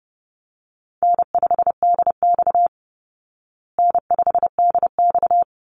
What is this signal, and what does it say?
Morse code, heard as a keyed tone, about 30 wpm, 700 Hz. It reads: N5DX N5DX
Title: Easy Call Signs Repeated 2x - 30wpm